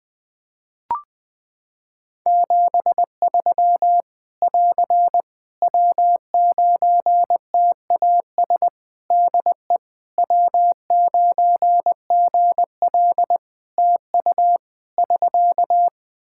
Morse code, heard as a keyed tone, about 20 wpm, 700 Hz. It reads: 73 <AR> W9TAS DE W9GL TU <SK>
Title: Practice QSO - Exchange 4 - Ending - 20wpm